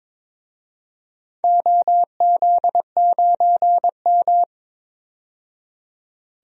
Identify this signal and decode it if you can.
Morse code, heard as a keyed tone, about 22 wpm, 700 Hz. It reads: OZ9M
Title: DXCC-100 Call Signs - 4 Characters - 22wpm